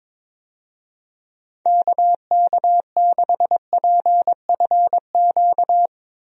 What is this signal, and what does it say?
Morse code, heard as a keyed tone, about 22 wpm, 700 Hz. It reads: KK6PFQ
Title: USA Call Signs - 6 Characters - Repeated 3x - 22wpm